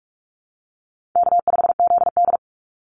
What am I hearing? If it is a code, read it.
Morse code, heard as a keyed tone, about 44 wpm, 700 Hz. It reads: K5ZD
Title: Easy Call Signs Repeated 3x - 45wpm